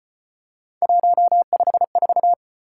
Morse code, heard as a keyed tone, about 36 wpm, 700 Hz. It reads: 154